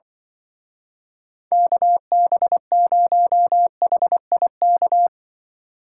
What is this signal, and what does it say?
Morse code, heard as a keyed tone, about 24 wpm, 700 Hz. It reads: KB0HIK
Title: Speed Racing - USA 2x3 Call Signs 20wpm